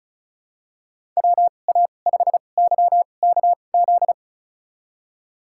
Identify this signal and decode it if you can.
Morse code, heard as a keyed tone, about 36 wpm, spaced as for 25 wpm, 700 Hz. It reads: WA5YKZ
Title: USA Call Signs - 6 Characters - 2x Character Spacing - 35wpm